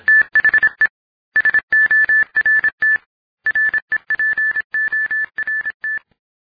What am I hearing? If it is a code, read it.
Morse code, heard as a keyed tone, about 26 wpm, 1750 Hz. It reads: THE SORT REPORT